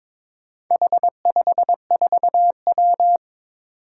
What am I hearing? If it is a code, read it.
Morse code, heard as a keyed tone, about 22 wpm, 700 Hz. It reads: H54W
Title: Four Challenging Letter-Number Combinations 22wpm